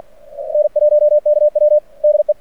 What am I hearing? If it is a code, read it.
Morse code, heard as a keyed tone, about 46 wpm, 610 Hz. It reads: N1KW DE
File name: archived / CRSnet